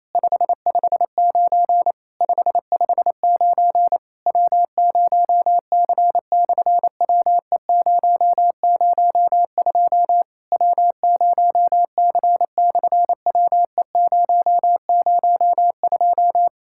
Morse code, heard as a keyed tone, about 28 wpm, 700 Hz. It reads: 559 559 W0C/WE002 W0C/WE002
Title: SOTA QSO Practice 28wpm